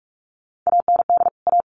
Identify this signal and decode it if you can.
Morse code, heard as a keyed tone, about 44 wpm, 700 Hz. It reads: AND A
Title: Sets of 2 Words 45wpm